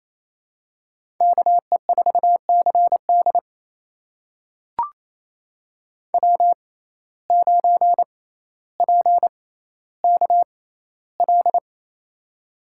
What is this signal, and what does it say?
Morse code, heard as a keyed tone, about 28 wpm, spaced as for 14 wpm, 700 Hz. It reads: KE4CD W9PKL
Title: USA Call Signs - 5 Characters - 6x Character Spacing - 28wpm